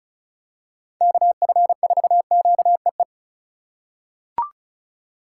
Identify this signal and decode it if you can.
Morse code, heard as a keyed tone, about 36 wpm, 700 Hz. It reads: KF4QEE
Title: USA Call Signs - 6 Characters - 8x Character Spacing - 35wpm